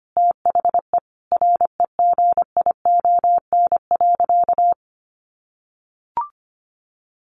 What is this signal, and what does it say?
Morse code, heard as a keyed tone, about 25 wpm, 700 Hz. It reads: THE REGION.
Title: Sentences from Top 1500 Words 25wpm